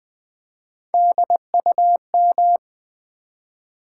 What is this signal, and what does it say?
Morse code, heard as a keyed tone, about 20 wpm, 700 Hz. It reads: DUM
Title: Build A Word - 1 Letter At A Time - 3 to 5-Letter Words - 20wpm